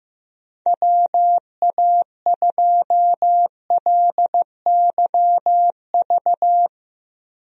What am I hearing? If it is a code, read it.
Morse code, heard as a keyed tone, about 15 wpm, 700 Hz. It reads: WA2LYV
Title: USA Call Signs - 6 Characters - 15wpm